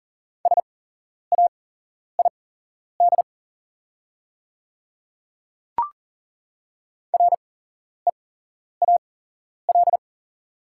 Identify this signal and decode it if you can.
Morse code, heard as a keyed tone, about 40 wpm, spaced as for 11 wpm, 700 Hz. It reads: SAID REAL
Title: 8x Character Spacing - Top 300 Words - 40wpm